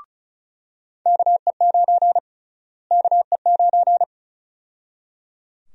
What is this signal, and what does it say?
Morse code, heard as a keyed tone, about 36 wpm, 700 Hz. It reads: KE9 KE9
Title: USA Call Sign Prefixes - Repeated 2x - 35wpm